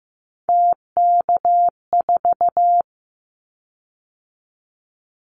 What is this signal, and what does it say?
Morse code, heard as a keyed tone, about 15 wpm, 700 Hz. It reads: TK4
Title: Callsign Prefixes for Top 100 DXCC - 15wpm